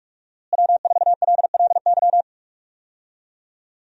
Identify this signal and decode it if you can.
Morse code, heard as a keyed tone, about 46 wpm, 700 Hz. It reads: W4LLY
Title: USA Call Signs - 5 Characters - Repeated 2x - 45wpm